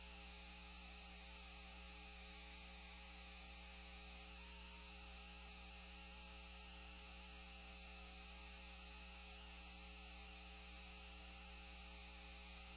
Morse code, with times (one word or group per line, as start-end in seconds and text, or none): none